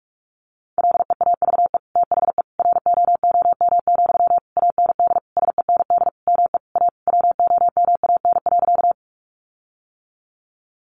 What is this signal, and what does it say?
Morse code, heard as a keyed tone, about 44 wpm, 700 Hz. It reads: LEAVE THE ROOM, AND SEND ME A WOMAN.
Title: Sentences from Top 1500 Words 45wpm